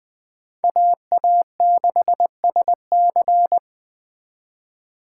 Morse code, heard as a keyed tone, about 20 wpm, 700 Hz. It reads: AA6SC